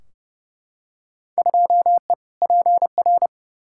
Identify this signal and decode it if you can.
Morse code, heard as a keyed tone, about 30 wpm, 700 Hz. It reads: 2E PR
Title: Field Day Exchanges Encore - 30wpm